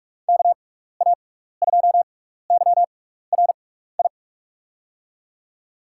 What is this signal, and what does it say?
Morse code, heard as a keyed tone, about 44 wpm, spaced as for 15 wpm, 700 Hz. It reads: KA2YRI
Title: USA Call Signs - 6x Character Spacing - 45wpm